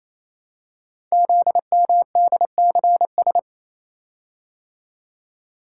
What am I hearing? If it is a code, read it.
Morse code, heard as a keyed tone, about 27 wpm, 700 Hz. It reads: ZMDCS
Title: Code Groups - 5 Letters - 28wpm